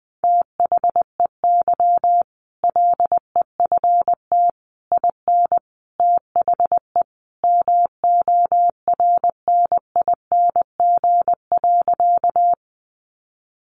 Morse code, heard as a keyed tone, about 19 wpm, 700 Hz. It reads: THEY LEFT IN THE MORNING.
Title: Sentences from Top 400 Words 20wpm